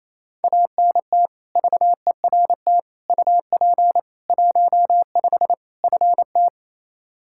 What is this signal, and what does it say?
Morse code, heard as a keyed tone, about 28 wpm, 700 Hz. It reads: ANT VERT UP 15 FT
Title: K1USN SST QSO Practice 28wpm